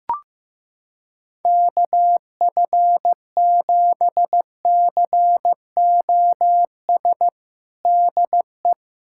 Morse code, heard as a keyed tone, about 15 wpm, 700 Hz. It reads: KF7COS DE
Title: USA Callsign DE Callsign - 6 Characters - 15wpm